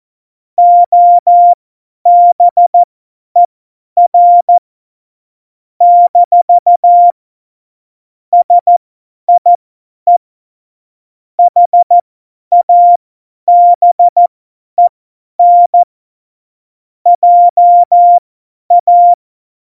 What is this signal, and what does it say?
Morse code, heard as a keyed tone, about 14 wpm, spaced as for 10 wpm, 700 Hz. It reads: OBER - SIE HABEN JA
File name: Morsekurs / Witze / Witz 04. Februar 2020, 50 F70